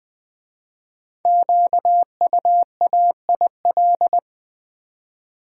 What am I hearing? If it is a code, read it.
Morse code, heard as a keyed tone, about 21 wpm, 700 Hz. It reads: QUAIL
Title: Difficult Words with Hints - 20wpm